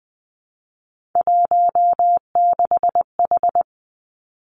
Morse code, heard as a keyed tone, about 19 wpm, 700 Hz. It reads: 16H
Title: Three Challenging Letter-Number Combinations 20wpm